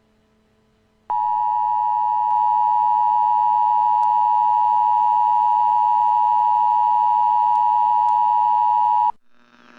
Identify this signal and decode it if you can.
Emergency Alert System (SAME) tones (the 853 and 960 Hz attention tone). It is heard from 1.0 to 9.2 s.
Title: Dead Reign EAS From President Kelly